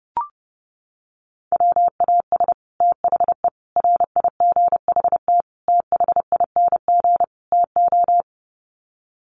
Morse code, heard as a keyed tone, about 30 wpm, 700 Hz. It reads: WAS THE RIGHT THING TO
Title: Sets of 5 Words Encore2 30wpm